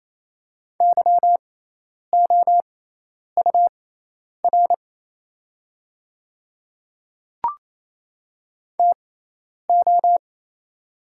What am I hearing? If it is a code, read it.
Morse code, heard as a keyed tone, about 28 wpm, spaced as for 10 wpm, 700 Hz. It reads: YOUR TO
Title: 6x Character Spacing - Top 100 Words - 28wpm